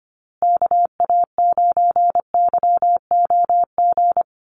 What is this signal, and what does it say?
Morse code, heard as a keyed tone, about 25 wpm, 700 Hz. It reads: KA9YOG
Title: USA Call Signs Repeated 2x 25wpm